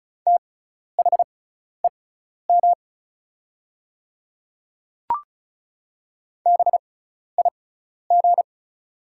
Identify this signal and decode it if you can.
Morse code, heard as a keyed tone, about 36 wpm, spaced as for 12 wpm, 700 Hz. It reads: THEM BIG
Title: 6x Character Spacing - Top 300 Words - 35wpm